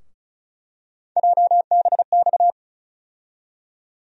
Morse code, heard as a keyed tone, about 36 wpm, 700 Hz. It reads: JBX
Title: USA Call Sign Postfixes - 3 Letters 35wpm